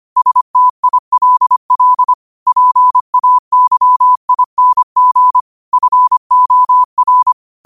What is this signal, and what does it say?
Morse code, heard as a keyed tone, about 25 wpm, 1000 Hz. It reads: STILL PAYING FOR
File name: qotdfiles / 25wpm